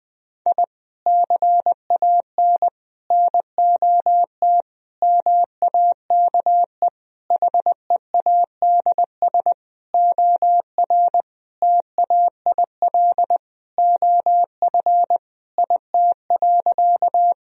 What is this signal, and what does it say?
Morse code, heard as a keyed tone, about 21 wpm, 700 Hz. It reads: I CAN NOT MAKE HEADS OR TAIL OF IT.
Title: Sentences from Top 1000 Words - Review Part 3 - 20wpm